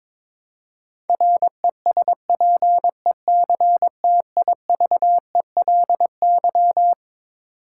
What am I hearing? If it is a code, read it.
Morse code, heard as a keyed tone, about 22 wpm, 700 Hz. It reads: RESPECTIVELY
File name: Twelve Letter Words - 22wpm